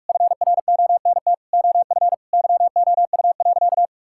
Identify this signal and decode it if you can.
Morse code, heard as a keyed tone, about 41 wpm, 700 Hz. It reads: FRONT OF YOU.
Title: Sentences from Top 1000 Words - Review Part 6 - 45wpm